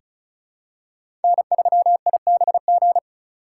Morse code, heard as a keyed tone, about 36 wpm, 700 Hz. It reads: N3IBG
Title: USA Call Signs - 5 Characters - 4x Character Spacing - 35wpm